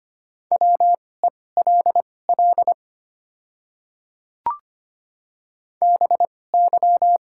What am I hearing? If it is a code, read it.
Morse code, heard as a keyed tone, about 25 wpm, spaced as for 18 wpm, 700 Hz. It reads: WELL BY
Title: 2x Character Spacing - Top 200 Words - 25wpm